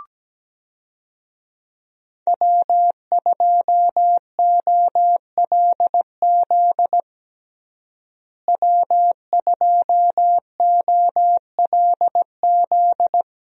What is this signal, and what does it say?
Morse code, heard as a keyed tone, about 17 wpm, 700 Hz. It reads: W2OLZ W2OLZ
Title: USA Call Signs - 5 Characters - Repeated 3x - 17wpm